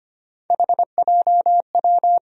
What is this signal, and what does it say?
Morse code, heard as a keyed tone, about 25 wpm, 700 Hz. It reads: HJW
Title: USA Call Sign Postfixes - 3 Letters 25wpm